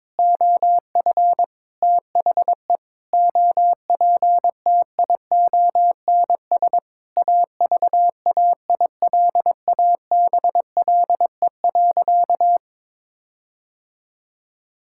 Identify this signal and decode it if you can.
Morse code, heard as a keyed tone, about 22 wpm, 700 Hz. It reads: OF THE OPTIONS AVAILABLE.
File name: Sentences with 7-Word Phrases - 22wpm